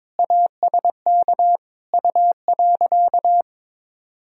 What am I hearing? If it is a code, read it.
Morse code, heard as a keyed tone, about 22 wpm, 700 Hz. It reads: ASK U.
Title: Sentences from Top 400 Words and Abbreviations 22wpm